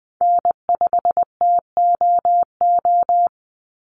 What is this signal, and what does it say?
Morse code, heard as a keyed tone, about 20 wpm, 700 Hz. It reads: N5TOO
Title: Call Signs Repeated 3x - 20wpm